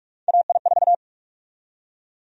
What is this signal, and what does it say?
Morse code, heard as a keyed tone, about 46 wpm, 700 Hz. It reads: AI4
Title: USA Call Sign Prefixes - 45wpm